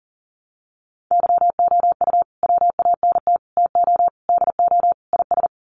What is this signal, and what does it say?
Morse code, heard as a keyed tone, about 40 wpm, 700 Hz. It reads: YOU WANT TO DO IS
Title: Sets of 5 Words Encore2 40wpm